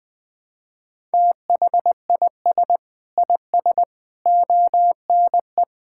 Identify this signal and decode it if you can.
Morse code, heard as a keyed tone, about 21 wpm, 700 Hz. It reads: THIS IS ONE
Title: Sentences with 5-Word Phrases - 20wpm